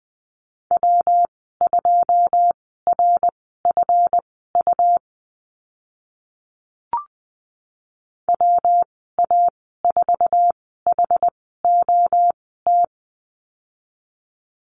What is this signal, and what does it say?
Morse code, heard as a keyed tone, about 20 wpm, spaced as for 15 wpm, 700 Hz. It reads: W2RFU WA4HOT
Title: 140 Lesson 23g - Letter f - Call Signs 20wpm